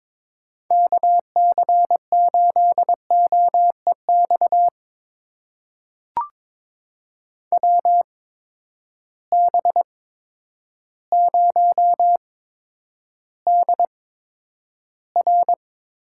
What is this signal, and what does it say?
Morse code, heard as a keyed tone, about 22 wpm, spaced as for 10 wpm, 700 Hz. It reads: KC8OEX WB0DR
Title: USA Call Signs - 6 Characters - 8x Character Spacing - 22wpm